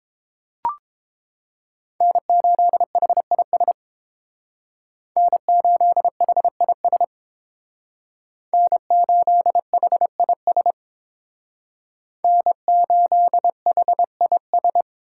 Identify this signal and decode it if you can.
Morse code, heard as a keyed tone, about 28 wpm, 700 Hz. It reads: N8HIS N8HIS N8HIS N8HIS